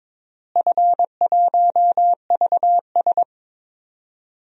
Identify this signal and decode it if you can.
Morse code, heard as a keyed tone, about 22 wpm, 700 Hz. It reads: F1VS
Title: DXCC-100 Call Signs - 4 Characters - Repeated 2x - 22wpm